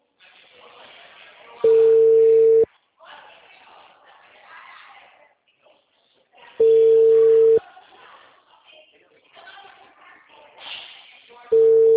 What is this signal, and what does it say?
Ringback tone (China, 450 Hz), with speech.